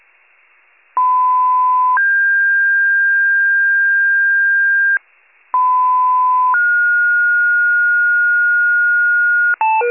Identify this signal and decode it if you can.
Two-tone page, heard as a one-second tone, then a three-second tone, about 1010 then 1600 Hz, about 1010 then 1430 Hz, with noise. It is heard from 1.0 to 9.5 s.
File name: wcalert